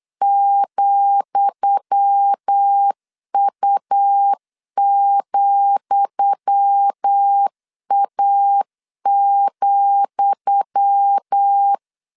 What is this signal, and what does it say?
Morse code, heard as a keyed tone, about 8 wpm, 800 Hz. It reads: ,U,A,
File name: LEZIONI / 10 SIGNS / 10D COMA 8wpm 3749